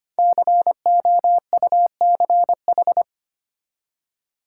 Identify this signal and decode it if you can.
Morse code, heard as a keyed tone, about 25 wpm, 700 Hz. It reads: COUCH
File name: practice / 25wpm / Easy Words with Hints - 25wpm